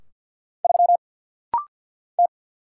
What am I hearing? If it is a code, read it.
Morse code, heard as a keyed tone, about 50 wpm, 690 Hz. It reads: 3 T
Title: Single Letter-Number - Rapid-Fire 50wpm